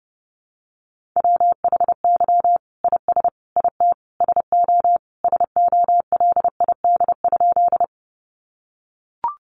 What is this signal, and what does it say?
Morse code, heard as a keyed tone, about 30 wpm, 700 Hz. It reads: WHY IS IT SO SOLID?
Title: Sentences from Top 1500 Words 30wpm